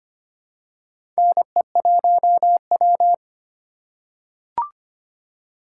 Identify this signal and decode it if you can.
Morse code, heard as a keyed tone, about 25 wpm, 700 Hz. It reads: NE1W
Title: USA Call Signs - 4 Characters - 3x Character Spacing - 25wpm